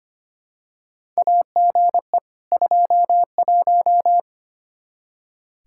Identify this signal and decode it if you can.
Morse code, heard as a keyed tone, about 25 wpm, 700 Hz. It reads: AGE 21
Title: Practice QSO - Exchange 3 - More Info - 25wpm